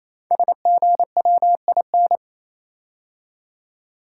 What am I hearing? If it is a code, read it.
Morse code, heard as a keyed tone, about 28 wpm, 700 Hz. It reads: SGWIN
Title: Code Groups - 5 Letters - 28wpm